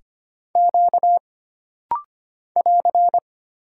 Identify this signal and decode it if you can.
Morse code, heard as a keyed tone, about 25 wpm, 700 Hz. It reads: Q <AR>
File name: Single Character (Letter:Number:Punctuation:Prosign) - Rapid-Fire - 25wpm